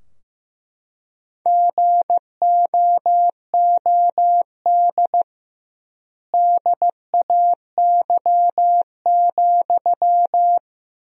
Morse code, heard as a keyed tone, about 15 wpm, 700 Hz. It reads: GOOD DAY,
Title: 2x Word-Spacing - Sentences from Top 200 Words - 15wpm